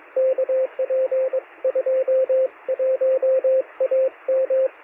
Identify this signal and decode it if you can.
Morse code, heard as a keyed tone, about 22 wpm, 520 Hz, with radio noise. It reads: KP21AM
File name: sm7lcb / 2114z oh3lwp 13cm